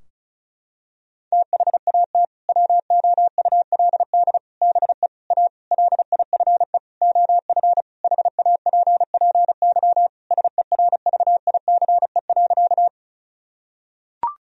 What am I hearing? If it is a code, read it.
Morse code, heard as a keyed tone, about 34 wpm, 700 Hz. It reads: THAT WOULD BE A LIFE OF HAPPY SERVICE.
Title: Sentences from Top 1000 Words - Review Part 4 - 35wpm